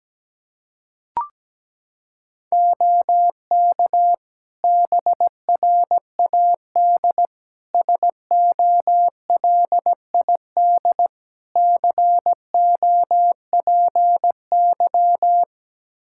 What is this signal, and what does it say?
Morse code, heard as a keyed tone, about 17 wpm, 700 Hz. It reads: OK BRAD SOLID COPY
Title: K1USN SST QSO Practice 17wpm